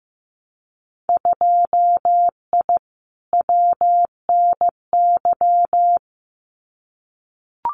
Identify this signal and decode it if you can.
Morse code, heard as a keyed tone, about 15 wpm, 700 Hz. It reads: 2I WNY
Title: Winter Field Day Exchanges 15wpm